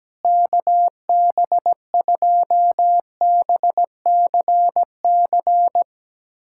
Morse code, heard as a keyed tone, about 17 wpm, 700 Hz. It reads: KB2BCC
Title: USA Callsign DE Callsign - 6 Characters - 17wpm